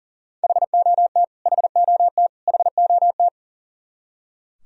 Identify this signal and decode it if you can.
Morse code, heard as a keyed tone, about 37 wpm, 700 Hz. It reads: HOT HOT HOT
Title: Top 100 Words Repeated 40wpm